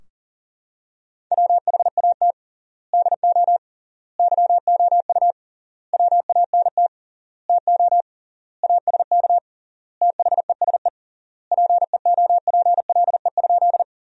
Morse code, heard as a keyed tone, about 40 wpm, 700 Hz. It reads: WHAT DO YOU WANT TO ASK THESE PEOPLE?
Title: 3x Word-Spacing - Sentences from Top 200 Words - 40wpm